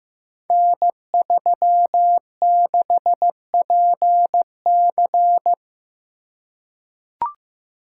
Morse code, heard as a keyed tone, about 15 wpm, 700 Hz. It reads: N36PC
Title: Code Groups - 5 Letter:Numbers - 15wpm